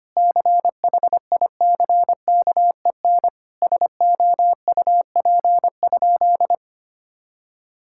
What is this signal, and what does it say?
Morse code, heard as a keyed tone, about 25 wpm, 700 Hz. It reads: CHICKEN SOUP?
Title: Sentences from Top 1000 Words - Review Part 6 - 25wpm